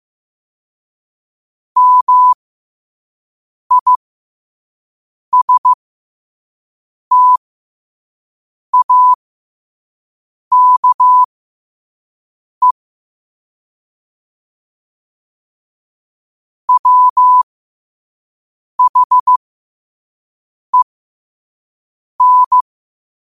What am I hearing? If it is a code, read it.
Morse code, heard as a keyed tone, about 15 wpm, spaced as for 5 wpm, 1000 Hz. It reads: MISTAKE WHEN
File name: qotdfiles / 5wpm